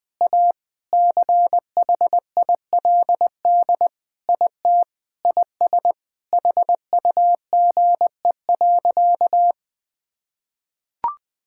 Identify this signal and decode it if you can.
Morse code, heard as a keyed tone, about 20 wpm, 700 Hz. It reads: A CHILD IT IS HUGE.
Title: Sentences with 5-Word Phrases - Spoken Beginning - 20wpm